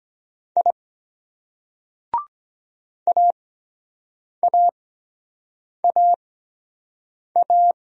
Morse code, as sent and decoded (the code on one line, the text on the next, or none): .. / .- .- .- .-
I AAAA